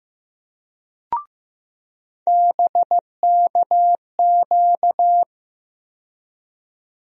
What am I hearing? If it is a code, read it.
Morse code, heard as a keyed tone, about 16 wpm, 700 Hz. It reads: BKQ
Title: USA Call Sign Postfixes - 3 Letters 15wpm